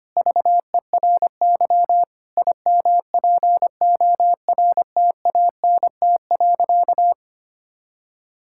Morse code, heard as a keyed tone, about 25 wpm, 700 Hz. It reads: VERY IMPORTANT.
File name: practice / 25wpm / Sentences with 4-Word Phrases - 25wpm